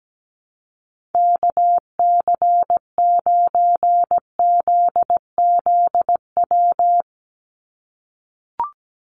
Morse code, heard as a keyed tone, about 17 wpm, 700 Hz. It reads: KC9ZZW
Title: USA Call Signs 17wpm